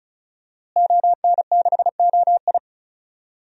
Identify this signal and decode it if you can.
Morse code, heard as a keyed tone, about 36 wpm, 700 Hz. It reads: ON6OI
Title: DXCC-100 Call Signs - 5 Characters - Repeated 2x - 35wpm